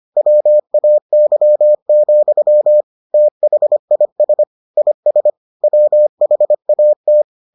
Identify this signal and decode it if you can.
Morse code, heard as a keyed tone, about 25 wpm, 600 Hz. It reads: WAY, THIS IS WHAT